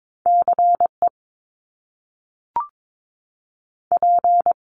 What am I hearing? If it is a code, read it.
Morse code, heard as a keyed tone, about 22 wpm, 700 Hz. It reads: CE P
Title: Top Two Letter Combinations in Words 22wpm